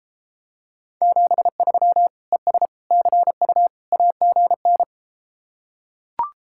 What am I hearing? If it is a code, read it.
Morse code, heard as a keyed tone, about 34 wpm, 700 Hz. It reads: 73 ES CU AGN